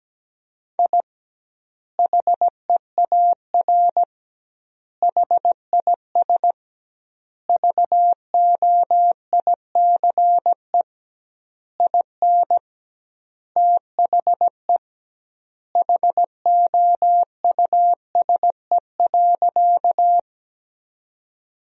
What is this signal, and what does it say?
Morse code, heard as a keyed tone, about 17 wpm, 700 Hz. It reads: I HEAR HIS VOICE IN THE HOUSE.
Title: 2x Word-Spacing - Sentences from Top 500 Words - 17wpm